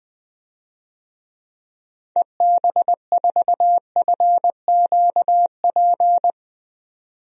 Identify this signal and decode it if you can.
Morse code, heard as a keyed tone, about 20 wpm, 700 Hz. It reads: EB4FQP
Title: DXCC-100 Call Signs - 6 Characters - 20wpm